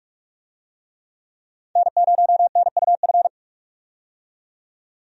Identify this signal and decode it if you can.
Morse code, heard as a keyed tone, about 46 wpm, 700 Hz. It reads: N0NUF
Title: USA Call Signs - 5 Characters - 45wpm